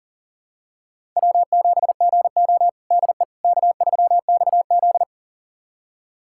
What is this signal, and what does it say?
Morse code, heard as a keyed tone, about 40 wpm, 700 Hz. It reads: W7GO DE K3XZ
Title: USA Callsign DE Callsign - 4 Characters - 40wpm